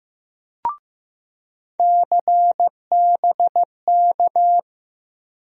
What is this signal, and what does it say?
Morse code, heard as a keyed tone, about 15 wpm, 700 Hz. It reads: CBK
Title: USA Call Sign Postfixes - 3 Letters 15wpm